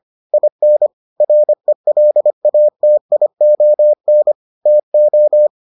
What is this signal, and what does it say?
Morse code, heard as a keyed tone, about 25 wpm, 600 Hz. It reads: IN RELATION TO